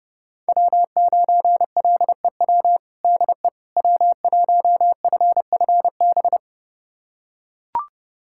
Morse code, heard as a keyed tone, about 32 wpm, 700 Hz. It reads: W9LEW DE W1FFB
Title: USA Callsign DE Callsign - 5 Characters - 30wpm